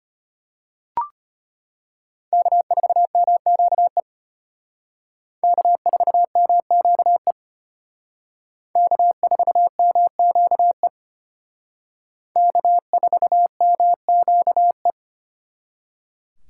Morse code, heard as a keyed tone, about 32 wpm, 700 Hz. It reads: K4MQE K4MQE K4MQE K4MQE